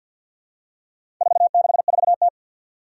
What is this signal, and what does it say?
Morse code, heard as a keyed tone, about 52 wpm, 700 Hz. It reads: 464T
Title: Four Challenging Letter-Number Combinations 50wpm